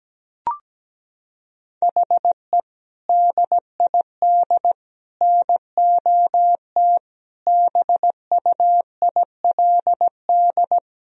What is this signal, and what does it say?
Morse code, heard as a keyed tone, about 17 wpm, 700 Hz. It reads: HE DID NOT BUILD
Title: Sentences from Top 1000 Words - Review Part 1 - 17wpm